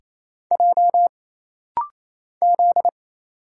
Morse code, heard as a keyed tone, about 28 wpm, 700 Hz. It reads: J Z
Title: Single Letters - Rapid-Fire 28wpm